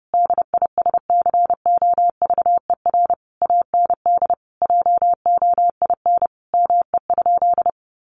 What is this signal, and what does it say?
Morse code, heard as a keyed tone, about 30 wpm, 700 Hz. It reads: DISCOVER AND JOIN ME?
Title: Sentences from Top 1400 Words 30wpm